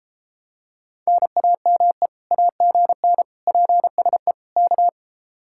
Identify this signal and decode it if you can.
Morse code, heard as a keyed tone, about 33 wpm, 700 Hz. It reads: NAME AGN PSE K